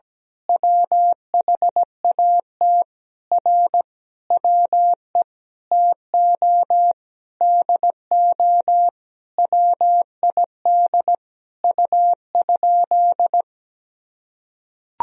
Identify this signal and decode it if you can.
Morse code, heard as a keyed tone, about 17 wpm, 700 Hz. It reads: WHAT R WE TO DO WID U?
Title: Sentences from Top 100 Words and Abbreviations 17wpm